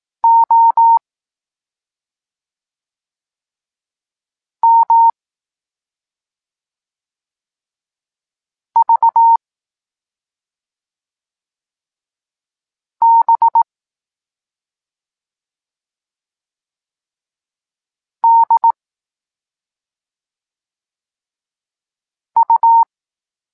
Morse code, heard as a keyed tone, about 18 wpm, 920 Hz. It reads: O M V B D U